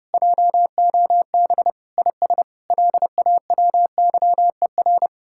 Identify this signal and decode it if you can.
Morse code, heard as a keyed tone, about 40 wpm, spaced as for 35 wpm, 700 Hz. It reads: JOB IS LAWYER
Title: Speed-Racing Common QSO Phrases Encore - 30wpm